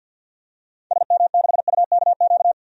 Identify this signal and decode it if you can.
Morse code, heard as a keyed tone, about 52 wpm, 700 Hz. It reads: SM6UKQ
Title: DXCC-100 Call Signs - 6 Characters - 50wpm